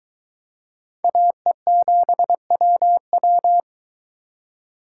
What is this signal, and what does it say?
Morse code, heard as a keyed tone, about 23 wpm, 700 Hz. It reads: AE7WW